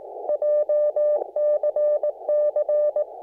Morse code, heard as a keyed tone, about 19 wpm, 600 Hz. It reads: JCC
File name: sdr-de-bcl / FT-991A JCC-9c861